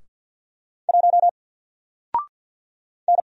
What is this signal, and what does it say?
Morse code, heard as a keyed tone, about 46 wpm, 700 Hz. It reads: 1 N
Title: Single Letter-Number 50wpm